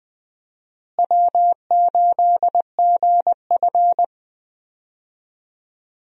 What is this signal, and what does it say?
Morse code, heard as a keyed tone, about 20 wpm, 700 Hz. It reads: W8GF
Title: USA Call Signs - 4 Characters - Repeated 2x - 20wpm